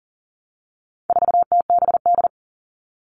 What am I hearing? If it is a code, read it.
Morse code, heard as a keyed tone, about 37 wpm, 700 Hz. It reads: 4TBD
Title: All Cut Numbers - 4-Digit 40wpm